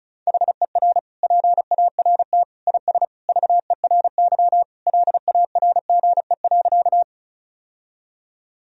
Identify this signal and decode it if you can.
Morse code, heard as a keyed tone, about 36 wpm, 700 Hz. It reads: HER PART IS VERY LARGE.
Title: Sentences from Top 1000 Words - Review Part 1 - 35wpm